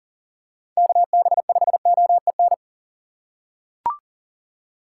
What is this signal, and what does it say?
Morse code, heard as a keyed tone, about 40 wpm, 700 Hz. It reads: KB5OEN
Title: USA Call Signs - 8x Character Spacing - 40wpm